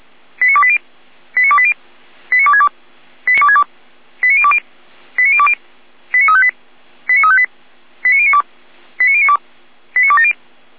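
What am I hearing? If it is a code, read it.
Five-tone selective call (ZVEI), heard as quick runs of five tones, tones of about 70 ms. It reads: 89280 89280 89262 89020 89020 89358 89358 89092 89092 89270
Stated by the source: ZVEI five-tone call received on an SDR.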